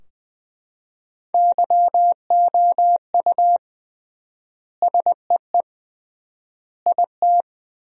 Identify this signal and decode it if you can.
Morse code, heard as a keyed tone, about 20 wpm, 700 Hz. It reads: YOU SEE IT
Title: 3x Word-Spacing - Sentences from Top 200 Words - 20wpm